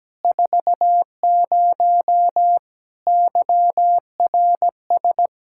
Morse code, heard as a keyed tone, about 17 wpm, 700 Hz. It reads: 40 YRS
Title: Practice QSO - Exchange 3 - More Info - 17wpm